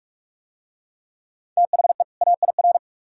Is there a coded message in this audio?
Yes: Morse code from 1.6 to 2.8 s, heard as a keyed tone, about 39 wpm, 700 Hz. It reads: THE AIR